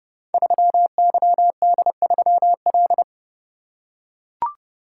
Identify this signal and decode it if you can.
Morse code, heard as a keyed tone, about 30 wpm, 700 Hz. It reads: 3YD3L
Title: Code Groups - 5 Letter:Numbers - 30wpm